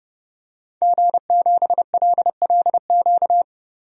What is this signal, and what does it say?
Morse code, heard as a keyed tone, about 30 wpm, 700 Hz. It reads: G7LLQ
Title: DXCC-100 Call Signs - 5 Characters - 30wpm